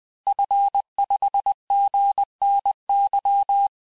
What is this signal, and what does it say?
Morse code, heard as a keyed tone, about 20 wpm, 800 Hz. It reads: F5GNY
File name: cours-du-mercredi / audios / cours-858-Fin-de-QSO-vitesse-19mpm